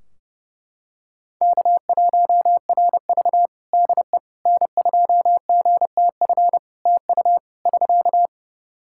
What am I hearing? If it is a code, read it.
Morse code, heard as a keyed tone, about 32 wpm, 700 Hz. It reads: K1RV DE N2GTF TU <SK>
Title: Practice QSO - Exchange 4 - Ending - 30wpm